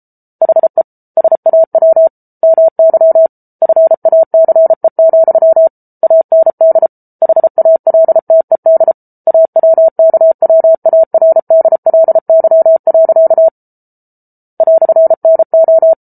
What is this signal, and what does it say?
Morse code, heard as a keyed tone, about 34 wpm, 650 Hz. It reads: HE SAW MY FACE, AND HALTED AWKWARDLY. "NO